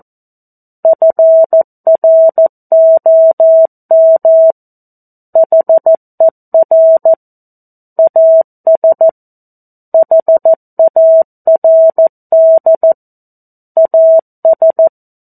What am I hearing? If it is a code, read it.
Morse code, heard as a keyed tone, about 14 wpm, 650 Hz. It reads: FROM HER AS HARD AS
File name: 02-Chapter-02 14 WPM